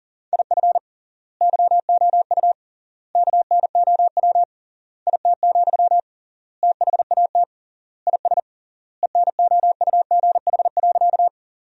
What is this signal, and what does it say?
Morse code, heard as a keyed tone, about 40 wpm, 700 Hz. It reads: IF YOU KNOW IT, THAT IS ENOUGH.
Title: 3x Word-Spacing - Sentences from Top 400 Words - 40wpm